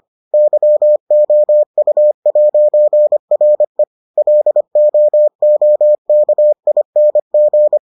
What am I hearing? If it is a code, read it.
Morse code, heard as a keyed tone, about 25 wpm, 600 Hz. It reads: YOU'RE LOOKING